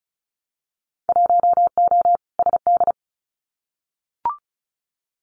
Morse code, heard as a keyed tone, about 36 wpm, 700 Hz. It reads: 1O SD
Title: Winter Field Day Exchanges Encore 35wpm